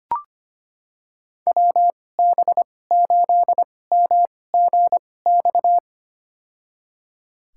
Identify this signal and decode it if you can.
Morse code, heard as a keyed tone, about 25 wpm, spaced as for 18 wpm, 700 Hz. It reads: WB8MGX
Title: USA Call Signs - 6 Characters - 2x Character Spacing - 25wpm